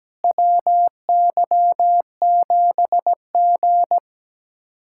Morse code, heard as a keyed tone, about 17 wpm, 700 Hz. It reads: WY7G
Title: USA Callsign DE Callsign - 4 Characters - 17wpm